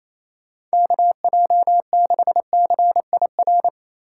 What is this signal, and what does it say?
Morse code, heard as a keyed tone, about 35 wpm, 700 Hz. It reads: KJ6CIR
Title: Speed Racing - USA 2x3 Call Signs 28wpm